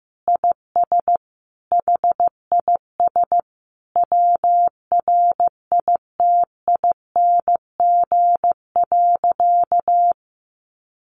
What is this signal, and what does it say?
Morse code, heard as a keyed tone, about 15 wpm, 700 Hz. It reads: IS HIS WRITING.
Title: Sentences from Top 800 Words 15wpm